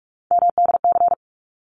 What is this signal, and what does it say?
Morse code, heard as a keyed tone, about 44 wpm, 700 Hz. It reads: MDC
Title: ARRL Sections 45wpm